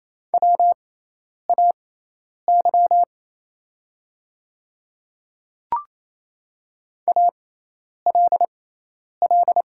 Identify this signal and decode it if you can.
Morse code, heard as a keyed tone, about 27 wpm, spaced as for 10 wpm, 700 Hz. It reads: WAY ALL
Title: 6x Character Spacing - Top 100 Words - 28wpm